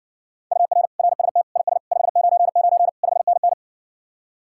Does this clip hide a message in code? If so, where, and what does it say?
Yes: Morse code from 0.5 to 3.5 s, heard as a keyed tone, about 60 wpm, 700 Hz. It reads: UR RST IS 599 5NN